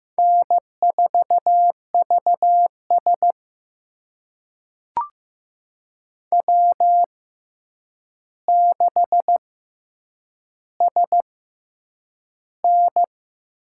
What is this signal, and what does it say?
Morse code, heard as a keyed tone, about 15 wpm, spaced as for 8 wpm, 700 Hz. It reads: N4VS W6SN
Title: USA Call Signs - 4 Characters - 6x Character Spacing - 15wpm